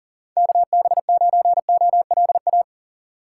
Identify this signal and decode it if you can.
Morse code, heard as a keyed tone, about 40 wpm, 700 Hz. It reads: KB9OLA
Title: USA Call Signs - 6x Character Spacing - 40wpm